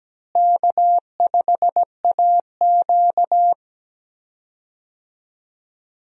Morse code, heard as a keyed tone, about 17 wpm, 700 Hz. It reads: K5AQ
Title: USA Call Signs - 4 Characters - Repeated 2x - 17wpm